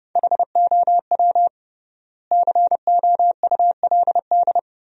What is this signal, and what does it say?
Morse code, heard as a keyed tone, about 30 wpm, 700 Hz. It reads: HOW COULD
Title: 3x Word-Spacing - Sentences from Top 200 Words - 30wpm